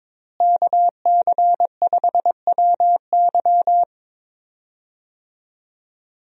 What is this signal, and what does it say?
Morse code, heard as a keyed tone, about 22 wpm, 700 Hz. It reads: KC5WY
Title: USA Call Signs - 5 Characters - Repeated 3x - 22wpm